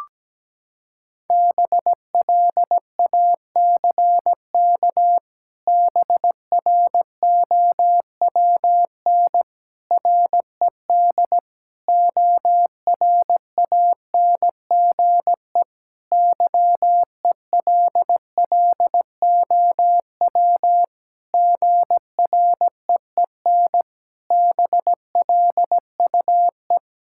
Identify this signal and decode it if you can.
Morse code, heard as a keyed tone, about 17 wpm, 700 Hz. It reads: BLACK BROWN RED ORANGE YELLOW GREEN BLUE
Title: Pangram Challenge v2 - 17wpm